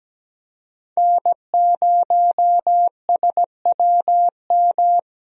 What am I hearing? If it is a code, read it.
Morse code, heard as a keyed tone, about 17 wpm, 700 Hz. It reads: N0SWM
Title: USA Call Signs - 5 Characters - Repeated 2x - 17wpm